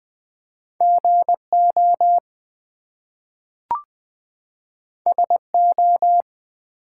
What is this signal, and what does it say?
Morse code, heard as a keyed tone, about 20 wpm, 700 Hz. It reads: GO SO
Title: Two Letter Words - 20wpm